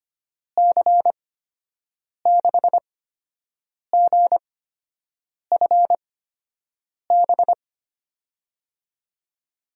Morse code, heard as a keyed tone, about 25 wpm, spaced as for 7 wpm, 700 Hz. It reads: C6GFB
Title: USA Call Signs - 6 Characters - 8x Character Spacing - 25wpm